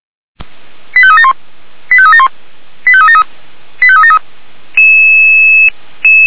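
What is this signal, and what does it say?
Five-tone selective call (ZVEI), heard as quick runs of five tones, tones of about 70 ms. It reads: 85371 85371 85373 85373
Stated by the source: ZVEI five-tone selective call 89180 received by SDR.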